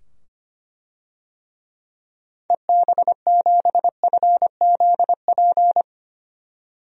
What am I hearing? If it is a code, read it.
Morse code, heard as a keyed tone, about 25 wpm, 700 Hz. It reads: EB7FZP
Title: DXCC-100 Call Signs - 6 Characters - 25wpm